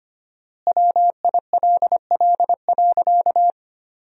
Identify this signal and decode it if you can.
Morse code, heard as a keyed tone, about 25 wpm, 700 Hz. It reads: WILL.
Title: 4x Word-Spacing - Sentences from Top 500 Words - 25wpm